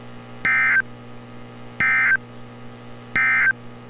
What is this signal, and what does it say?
Emergency Alert System (SAME) tones, heard as harsh buzzing bursts. It is heard from 0.3 to 3.6 s.
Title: NOAA Weather Radio - Altoona, PA